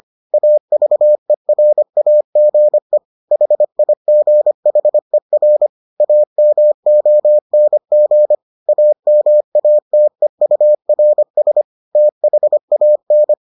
Morse code, heard as a keyed tone, about 25 wpm, 600 Hz. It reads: AVERAGE HIGHER AMONG AMATEURS THAN